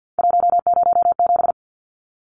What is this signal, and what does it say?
Morse code, heard as a keyed tone, about 50 wpm, 700 Hz. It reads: 107
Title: Numbers - 3-Digit 50wpm